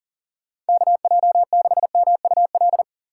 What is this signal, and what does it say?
Morse code, heard as a keyed tone, about 50 wpm, 700 Hz. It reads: KJ6MUL
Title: Speed Racing - USA 2x3 Call Signs 40wpm